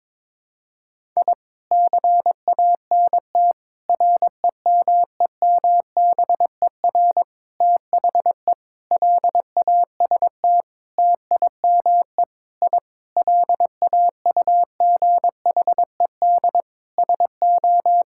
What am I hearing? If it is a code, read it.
Morse code, heard as a keyed tone, about 21 wpm, 700 Hz. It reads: I CANT REMEMBER THE LAST TIME I LAUGHED SO
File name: Sentences with 7-Word Phrases - 22wpm